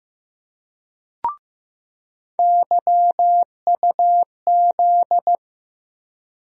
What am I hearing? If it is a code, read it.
Morse code, heard as a keyed tone, about 15 wpm, 700 Hz. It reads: YUZ
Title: USA Call Sign Postfixes - 3 Letters 15wpm